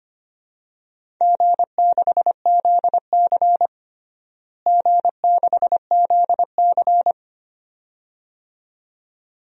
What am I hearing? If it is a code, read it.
Morse code, heard as a keyed tone, about 25 wpm, 700 Hz. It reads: G6ZC G6ZC
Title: DXCC-100 Call Signs - 4 Characters - Repeated 2x - 25wpm